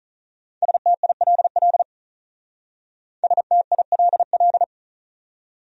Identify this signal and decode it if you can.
Morse code, heard as a keyed tone, about 44 wpm, 700 Hz. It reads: STILL STILL